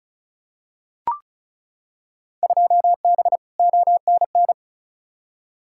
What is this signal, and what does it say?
Morse code, heard as a keyed tone, about 36 wpm, 700 Hz. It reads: 2B ONN
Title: Field Day Exchanges Encore - 35wpm